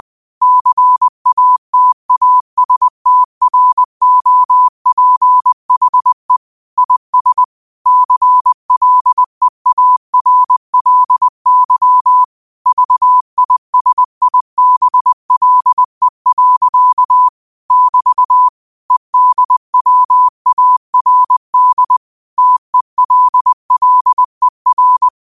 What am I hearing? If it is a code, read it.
Morse code, heard as a keyed tone, about 20 wpm, 1000 Hz. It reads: CATASTROPHE IS CLEARLY VISIBLE. <BT> EDWARD TELLER